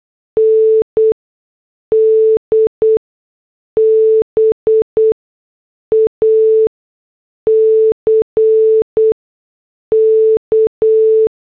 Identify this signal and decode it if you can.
Morse code, heard as a keyed tone, about 8 wpm, spaced as for 6 wpm, 440 Hz. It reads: ND BACK